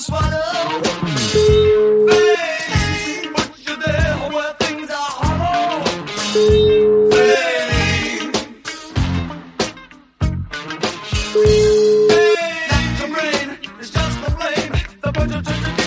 Ringback tone (Europe, 425 Hz), with noise.